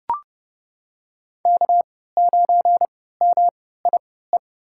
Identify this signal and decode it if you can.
Morse code, heard as a keyed tone, about 32 wpm, spaced as for 17 wpm, 700 Hz. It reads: K9MIE
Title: USA Call Signs - 5 Characters - 3x Character Spacing - 30wpm